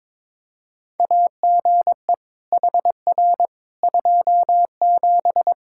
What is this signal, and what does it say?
Morse code, heard as a keyed tone, about 22 wpm, 700 Hz. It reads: AGE HR 27
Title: Practice QSO - Exchange 3 - More Info - 22wpm